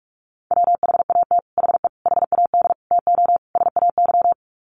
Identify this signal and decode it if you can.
Morse code, heard as a keyed tone, about 44 wpm, 700 Hz. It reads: WHAT HE HAD TO SAY
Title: Sets of 5 Words Encore2 45wpm